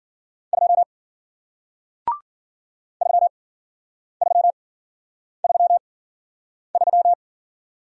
Morse code, heard as a keyed tone, about 50 wpm, spaced as for 9 wpm, 700 Hz. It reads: 2 3333